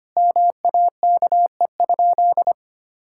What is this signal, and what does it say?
Morse code, heard as a keyed tone, about 25 wpm, 700 Hz. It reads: MAKE?
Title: Sentences from Top 1000 Words - Review Part 5 - 25wpm